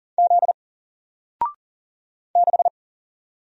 Morse code, heard as a keyed tone, about 40 wpm, 700 Hz. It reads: Z 6
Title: Single Character (Letter:Number:Punctuation:Prosign) Alt 40wpm